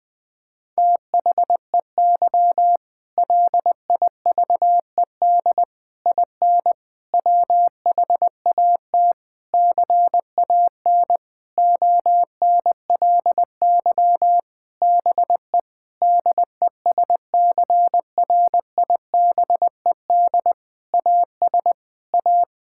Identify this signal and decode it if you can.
Morse code, heard as a keyed tone, about 20 wpm, 700 Hz. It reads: THEY LIVED IN WHAT CAN ONLY BE DESCRIBED AS A
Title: Sentences with 8-Word Phrases - Spoken Beginning - 20wpm